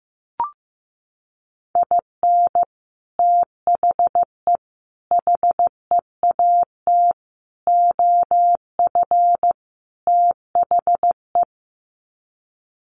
Morse code, heard as a keyed tone, about 15 wpm, 700 Hz. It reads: IN THE HEAT OF THE
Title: Sets of 5 Words Encore2 15wpm